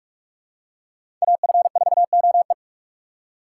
Morse code, heard as a keyed tone, about 47 wpm, 700 Hz. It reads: AF4OE
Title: USA Call Signs - 5 Characters - 6x Character Spacing - 45wpm